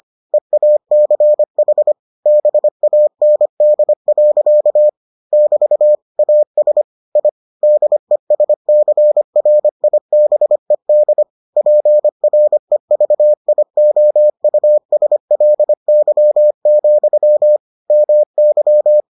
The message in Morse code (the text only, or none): EACH BAND. <BT> AS I DESCRIBED PREVIOUSLY, MY